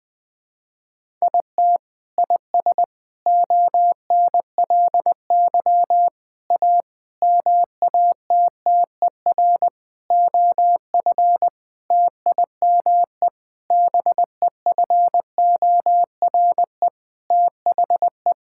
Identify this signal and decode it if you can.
Morse code, heard as a keyed tone, about 20 wpm, 700 Hz. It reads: IT IS ONLY A MATTER OF TIME BEFORE THE
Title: Sentences with 9-Word Phrases - 20wpm